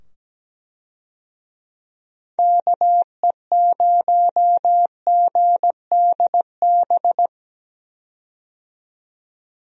Morse code, heard as a keyed tone, about 17 wpm, 700 Hz. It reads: KE0GDB
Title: USA Call Signs - 6 Characters - 17wpm